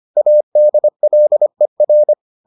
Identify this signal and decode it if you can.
Morse code, heard as a keyed tone, about 25 wpm, 600 Hz. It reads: ADLER